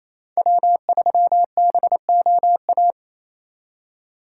Morse code, heard as a keyed tone, about 34 wpm, 700 Hz. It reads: W3BOA
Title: Speed Racing - USA 1x3 Call Signs 28wpm